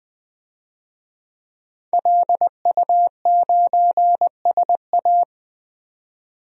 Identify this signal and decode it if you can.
Morse code, heard as a keyed tone, about 20 wpm, 700 Hz. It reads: LU9SA
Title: DXCC-100 Call Signs - 5 Characters - 20wpm